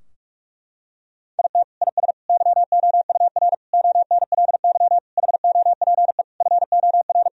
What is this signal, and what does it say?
Morse code, heard as a keyed tone, about 41 wpm, 700 Hz. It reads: IT IS YOUR ONLY HOPE FOR
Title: Sentences from Top 1000 Words - Review Part 4 - 45wpm